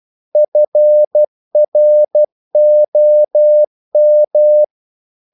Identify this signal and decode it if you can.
Morse code, heard as a keyed tone, about 12 wpm, 600 Hz. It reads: FROM
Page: English quotations in CW (telegraphy - morse)